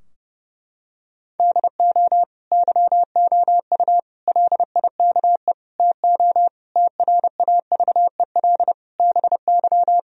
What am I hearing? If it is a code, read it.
Morse code, heard as a keyed tone, about 30 wpm, 700 Hz. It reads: DO YOU LIKE TO TRAVEL BY
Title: Sentences from Top 1000 Words - Review Part 3 - 30wpm